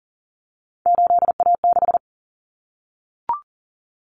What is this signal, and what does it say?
Morse code, heard as a keyed tone, about 40 wpm, 700 Hz. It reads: 8A6
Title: Common Cut Numbers - 3-Digit 40wpm